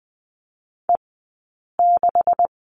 Morse code, heard as a keyed tone, about 20 wpm, spaced as for 8 wpm, 700 Hz. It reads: E6
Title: 158 Lesson 26d - Letter g - Two Characters 20wpm